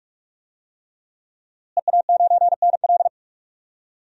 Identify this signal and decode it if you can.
Morse code, heard as a keyed tone, about 46 wpm, 700 Hz. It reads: EA9NL
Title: DXCC-100 Call Signs - 5 Characters - Repeated 3x - 45wpm